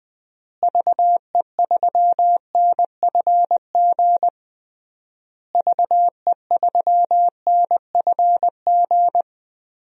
Morse code, heard as a keyed tone, about 20 wpm, 700 Hz. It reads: VE3NFG VE3NFG
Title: DXCC-100 Call Signs - 6 Characters - Repeated 3x - 20wpm